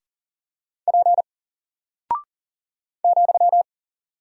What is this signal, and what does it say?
Morse code, heard as a keyed tone, about 40 wpm, 700 Hz. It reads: P ,
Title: Single Character (Letter:Number:Punctuation:Prosign) 40wpm